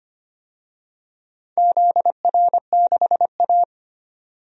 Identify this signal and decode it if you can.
Morse code, heard as a keyed tone, about 25 wpm, 700 Hz. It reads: ZR6A
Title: DXCC-100 Call Signs - 4 Characters - 25wpm